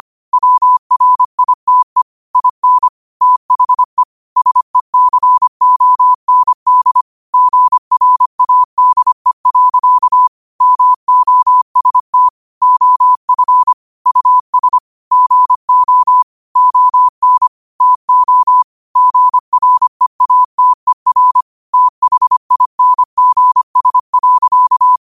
Morse code, heard as a keyed tone, about 25 wpm, 1000 Hz. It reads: WRITE IN THE SECOND GRADE. MOST OF US GO ON TO GREATER THINGS.